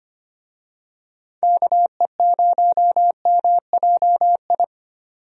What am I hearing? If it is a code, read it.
Morse code, heard as a keyed tone, about 25 wpm, 700 Hz. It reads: KE0MJI
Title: USA Call Signs - 6 Characters - 25wpm